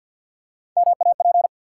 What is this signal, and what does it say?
Morse code, heard as a keyed tone, about 52 wpm, 700 Hz. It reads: MAP
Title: Three Letter Words - 50wpm